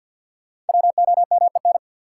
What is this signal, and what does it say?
Morse code, heard as a keyed tone, about 51 wpm, 700 Hz. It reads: WOMEN
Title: Sentences from Top 1300 Words - New Word Drill - 50wpm